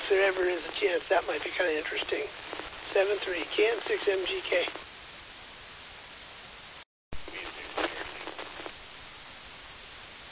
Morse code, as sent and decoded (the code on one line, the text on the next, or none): none